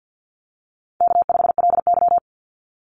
Morse code, heard as a keyed tone, about 50 wpm, 700 Hz. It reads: K5LY
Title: Easy Call Signs - 50wpm